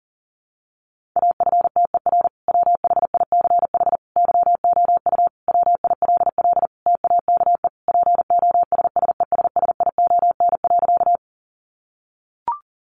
Morse code, heard as a keyed tone, about 40 wpm, 700 Hz. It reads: AFTER WHICH YOU WILL TAKE POSSESSION.
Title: Sentences from Top 1400 Words 40wpm